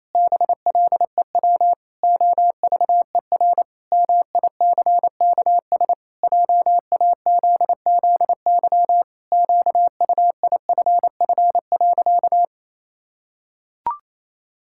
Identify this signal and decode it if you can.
Morse code, heard as a keyed tone, about 28 wpm, 700 Hz. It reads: BLEW OVER MICKS JAZZY QUIFF.
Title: Pangram Challenge v2 - 28wpm